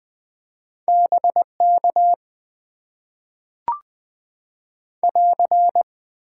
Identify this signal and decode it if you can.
Morse code, heard as a keyed tone, about 20 wpm, 700 Hz. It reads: BK <AR>
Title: Top 90 QSO Elements - 6x Character Spacing - 20wpm